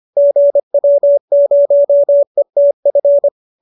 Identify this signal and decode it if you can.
Morse code, heard as a keyed tone, about 25 wpm, 570 Hz. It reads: GW0ETF